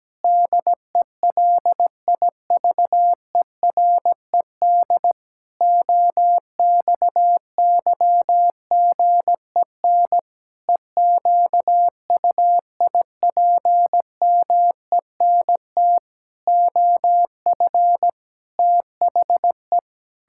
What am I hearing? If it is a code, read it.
Morse code, heard as a keyed tone, about 17 wpm, 700 Hz. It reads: DELIVERED OXYGEN EQUIPMENT OF THE
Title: Pangram Challenge v2 - 17wpm